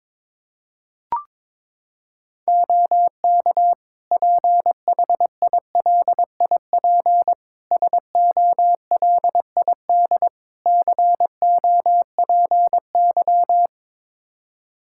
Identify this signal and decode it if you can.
Morse code, heard as a keyed tone, about 21 wpm, 700 Hz. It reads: OK PHILIP SOLID COPY
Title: K1USN SST QSO Practice 22wpm